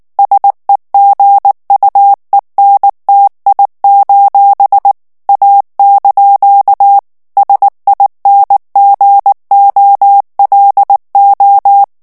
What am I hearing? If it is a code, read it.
Morse code, heard as a keyed tone, about 19 wpm, 800 Hz. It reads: SEGUENTI: A) SINGOLO
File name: morse mp3 / ITALIEN texte clair 2 vitesse 18 mots minute